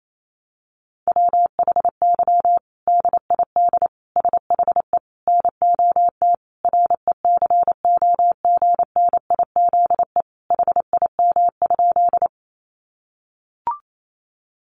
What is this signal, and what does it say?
Morse code, heard as a keyed tone, about 28 wpm, 700 Hz. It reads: WHY DID SHE NOT RECOGNIZE HIM?
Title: Sentences from Top 1400 Words 28wpm